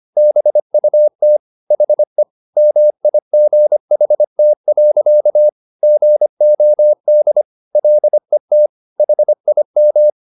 Morse code, heard as a keyed tone, about 25 wpm, 600 Hz. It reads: BUT HE MIGHT. GOD LET HIM
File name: the old man and the sea part 1 25wpm